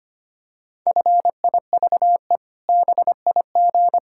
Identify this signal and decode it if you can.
Morse code, heard as a keyed tone, about 25 wpm, 700 Hz. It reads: FIVE BIG
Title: Pangram Challenge v2 - 25wpm